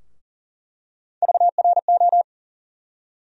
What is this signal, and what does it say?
Morse code, heard as a keyed tone, about 39 wpm, 700 Hz. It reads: VRO
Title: USA Call Sign Postfixes - 3 Letters 40wpm